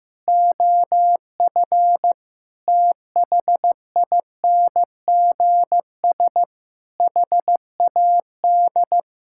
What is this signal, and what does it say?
Morse code, heard as a keyed tone, about 15 wpm, 700 Hz. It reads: OF THINGS HAD
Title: Sentences from Top 1000 Words - Review Part 6 - 15wpm